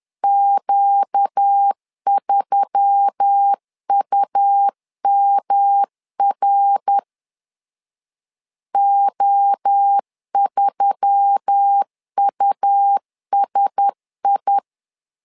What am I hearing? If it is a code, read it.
Morse code, heard as a keyed tone, about 10 wpm, 800 Hz. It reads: Q3UMR O3USI